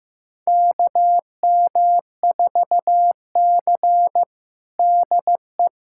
Morse code, heard as a keyed tone, about 15 wpm, 700 Hz. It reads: KM4C DE
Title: USA Callsign DE Callsign - 4 Characters - 15wpm